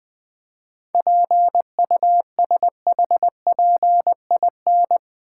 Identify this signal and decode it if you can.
Morse code, heard as a keyed tone, about 22 wpm, 700 Hz. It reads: PUSHPIN
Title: Easy Words - Speed-Racing with Hints - 20wpm